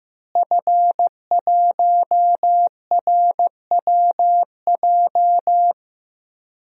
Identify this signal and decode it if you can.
Morse code, heard as a keyed tone, about 15 wpm, 700 Hz. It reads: F1RWJ
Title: DXCC-100 Call Signs - 5 Characters - Repeated 3x - 15wpm